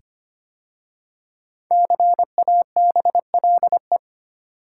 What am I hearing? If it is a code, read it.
Morse code, heard as a keyed tone, about 25 wpm, 700 Hz. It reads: CABLE
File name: 5 Letter Words - First 3 Letters Given - 25wpm